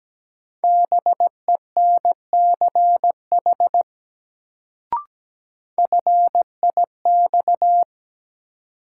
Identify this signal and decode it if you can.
Morse code, heard as a keyed tone, about 17 wpm, 700 Hz. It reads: BENCH FIX
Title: Sentences from Top 1400 Words - New Word Drill - 17wpm